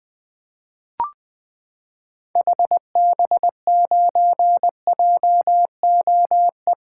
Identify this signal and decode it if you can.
Morse code, heard as a keyed tone, about 20 wpm, 700 Hz. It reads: HB9JOE
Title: 2023 SOTA Activator Call Signs - 20wpm